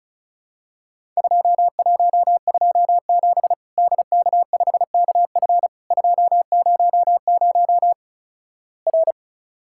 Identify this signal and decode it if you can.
Morse code, heard as a keyed tone, about 36 wpm, 700 Hz. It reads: 2127 DK5KF 200 R
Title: Worked All Europe DX Contest - QTC Exchange - 35wpm